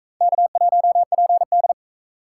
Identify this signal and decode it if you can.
Morse code, heard as a keyed tone, about 39 wpm, 700 Hz. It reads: K1PD